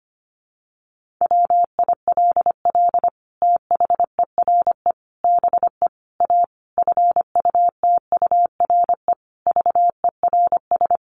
Morse code, heard as a keyed tone, about 25 wpm, 700 Hz. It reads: WILL THERE BE A FUTURE VERS
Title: Sentences from Top 1400 Words 25wpm